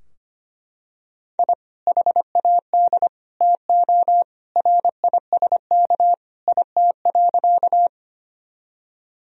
Morse code, heard as a keyed tone, about 25 wpm, 700 Hz. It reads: I HAD TO RISK IT.
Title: Sentences from Top 1000 Words - Review Part 4 - 25wpm